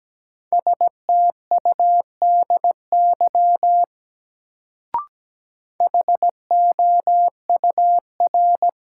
Morse code, heard as a keyed tone, about 17 wpm, 700 Hz. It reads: STUDY HOUR
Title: Sets of 5 Words - New Word Drill - 17wpm